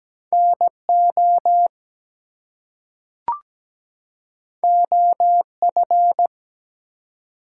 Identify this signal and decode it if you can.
Morse code, heard as a keyed tone, about 17 wpm, 700 Hz. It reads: NO OF
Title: Two Letter Words - 17wpm